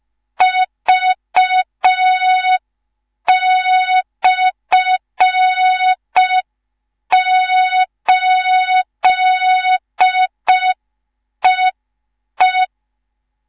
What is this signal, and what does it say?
Morse code, heard as a keyed tone, about 5 wpm, 3700 Hz. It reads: V/8EE
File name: Morze-HA2MN / hang / 19 GYAKORLAT